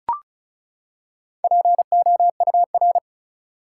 Morse code, heard as a keyed tone, about 36 wpm, 700 Hz. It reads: POUR
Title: Four Letter Words - 35wpm